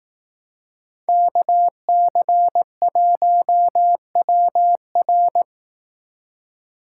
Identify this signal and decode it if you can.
Morse code, heard as a keyed tone, about 18 wpm, 700 Hz. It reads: KC1WR
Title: Speed Racing - USA 2x2 Call Signs 15wpm